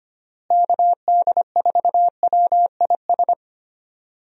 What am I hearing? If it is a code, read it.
Morse code, heard as a keyed tone, about 25 wpm, 700 Hz. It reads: KD4WIS
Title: USA Call Signs - 6 Characters - Repeated 2x - 25wpm